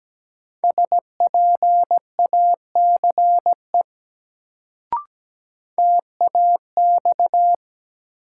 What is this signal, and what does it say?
Morse code, heard as a keyed tone, about 17 wpm, 700 Hz. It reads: SPACE TAX
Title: Sentences from Top 1000 Words - Review Part 5 - New Word Drill - 17wpm